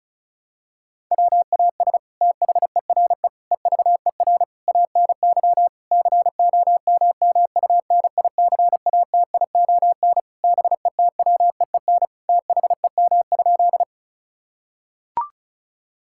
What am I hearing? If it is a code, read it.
Morse code, heard as a keyed tone, about 36 wpm, 700 Hz. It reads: WAS THERE EVER ANY COMMUNICATION BETWEEN THEM?
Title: Sentences from Top 1000 Words - Review Part 6 - 35wpm